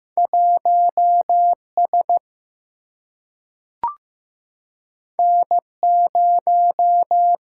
Morse code, heard as a keyed tone, about 15 wpm, 700 Hz. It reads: 1 S N 0
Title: Single-Letter Single-Number 15wpm